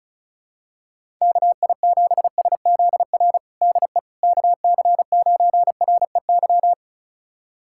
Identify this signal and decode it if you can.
Morse code, heard as a keyed tone, about 34 wpm, 700 Hz. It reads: KI7SZR DE KC9REY
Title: USA Callsign DE Callsign - 6 Characters - 35wpm